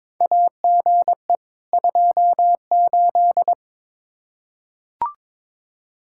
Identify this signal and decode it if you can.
Morse code, heard as a keyed tone, about 22 wpm, 700 Hz. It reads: AGE 28
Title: Practice QSO - Exchange 3 - More Info - 22wpm